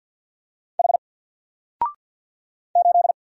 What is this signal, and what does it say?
Morse code, heard as a keyed tone, about 48 wpm, 700 Hz. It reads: H 8
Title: Single Letter-Number Alt 50wpm